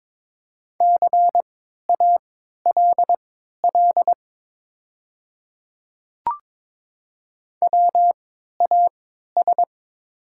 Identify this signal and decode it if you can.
Morse code, heard as a keyed tone, about 22 wpm, spaced as for 12 wpm, 700 Hz. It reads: CALL WAS
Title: 3x Character Spacing - Top 200 Words - 22wpm